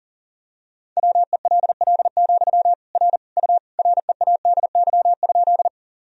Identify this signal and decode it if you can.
Morse code, heard as a keyed tone, about 40 wpm, 700 Hz. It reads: WELL, R U READY?
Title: Sentences from Top 400 Words and Abbreviations 40wpm